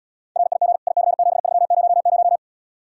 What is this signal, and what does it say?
Morse code, heard as a keyed tone, about 79 wpm, 700 Hz. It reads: RIG IC7300